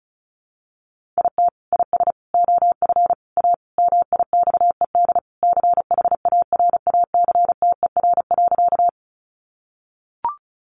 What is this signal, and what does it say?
Morse code, heard as a keyed tone, about 34 wpm, 700 Hz. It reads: IT IS OF A MIXED CHARACTER.
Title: Sentences from Top 1500 Words 35wpm